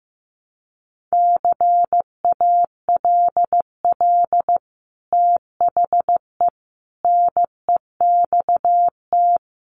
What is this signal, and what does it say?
Morse code, heard as a keyed tone, about 15 wpm, 700 Hz. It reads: CALL THE NEXT
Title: Sentences from Top 300 Words 15wpm